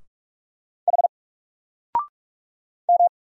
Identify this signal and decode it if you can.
Morse code, heard as a keyed tone, about 44 wpm, 700 Hz. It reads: H M
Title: Single Letter-Number Alt 45wpm